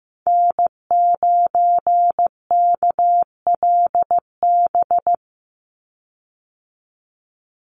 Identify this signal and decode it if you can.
Morse code, heard as a keyed tone, about 15 wpm, 700 Hz. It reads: N9KLB
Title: USA Call Signs 15wpm